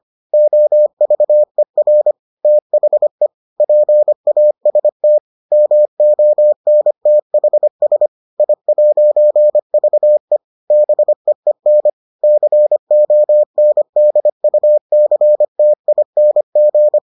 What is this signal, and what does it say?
Morse code, heard as a keyed tone, about 25 wpm, 600 Hz. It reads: OVER THE PAST MONTHS I'VE BEEN CONDUCTING